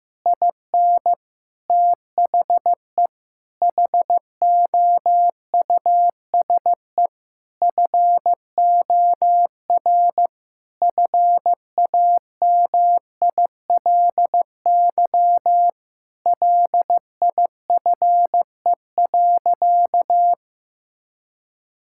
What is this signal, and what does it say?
Morse code, heard as a keyed tone, about 15 wpm, 700 Hz. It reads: IN THE HOUSE FOR FAMILY LIFE.
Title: Sentences from Top 1000 Words - Review Part 5 - 15wpm